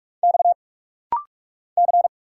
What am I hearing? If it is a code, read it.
Morse code, heard as a keyed tone, about 46 wpm, 700 Hz. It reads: X C
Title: Single Letter-Number Alt - Rapid-Fire 45wpm